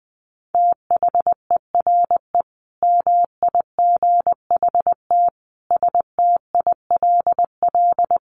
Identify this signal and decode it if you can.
Morse code, heard as a keyed tone, about 20 wpm, 700 Hz. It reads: THERE MIGHT STILL
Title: Sentences from Top 1100 Words 20wpm